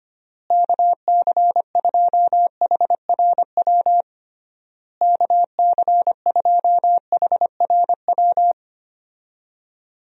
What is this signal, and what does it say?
Morse code, heard as a keyed tone, about 25 wpm, 700 Hz. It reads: KC2HRW KC2HRW
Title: USA Call Signs - 6 Characters - Repeated 3x - 25wpm